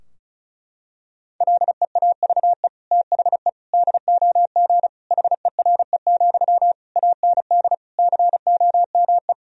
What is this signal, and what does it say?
Morse code, heard as a keyed tone, about 36 wpm, 700 Hz. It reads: LEAVE THE DOG HERE, AND COME
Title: Sentences from Top 1000 Words - Review Part 2 - 35wpm